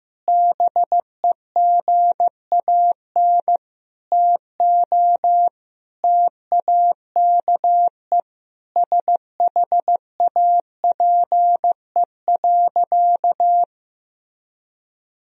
Morse code, heard as a keyed tone, about 15 wpm, 700 Hz. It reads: BEGAN TO TAKE SHAPE.
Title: Sentences from Top 1000 Words - Review Part 3 - 15wpm